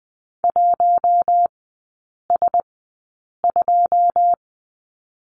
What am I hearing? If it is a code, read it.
Morse code, heard as a keyed tone, about 20 wpm, spaced as for 8 wpm, 700 Hz. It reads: 1S2
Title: 081 Lesson 14e - Number 2 - Three Characters 20wpm